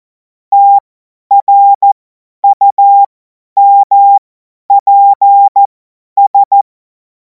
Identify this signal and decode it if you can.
Morse code, heard as a keyed tone, about 14 wpm, spaced as for 10 wpm, 800 Hz. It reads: TRUMPS
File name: Morsekurs / Witze / Witz 05. Mai 2026 50 F70